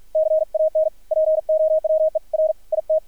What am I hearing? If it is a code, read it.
Morse code, heard as a keyed tone, about 53 wpm, spaced as for 46 wpm, 630 Hz, with radio noise. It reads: CAN POWER IT
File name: archived / CRSnet